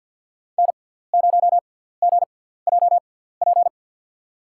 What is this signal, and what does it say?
Morse code, heard as a keyed tone, about 53 wpm, spaced as for 17 wpm, 700 Hz. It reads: N0GJP
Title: USA Call Signs - 5 Characters - 6x Character Spacing - 50wpm